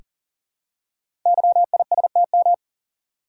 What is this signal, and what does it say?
Morse code, heard as a keyed tone, about 42 wpm, 690 Hz. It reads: YISTM
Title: Code Groups - 5 Letters - 40wpm